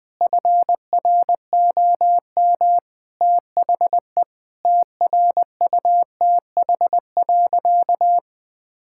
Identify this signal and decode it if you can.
Morse code, heard as a keyed tone, about 20 wpm, 700 Hz. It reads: FROM THE TRUTH.
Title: Sentences with 7-Word Phrases - 20wpm